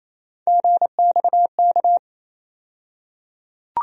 Morse code, heard as a keyed tone, about 28 wpm, 700 Hz. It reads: GXK
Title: USA Call Sign Postfixes - 3 Letters 28wpm